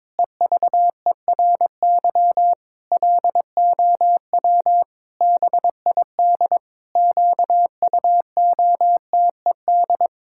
Morse code, heard as a keyed tone, about 22 wpm, 700 Hz. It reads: EVERY LOW BID QUOTED
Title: Pangram Challenge v2 - 22wpm